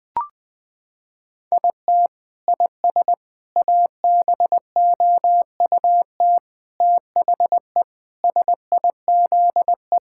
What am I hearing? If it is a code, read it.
Morse code, heard as a keyed tone, about 20 wpm, 700 Hz. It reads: IT IS ABOUT THE SIZE
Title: Sentences with 5-Word Phrases - 20wpm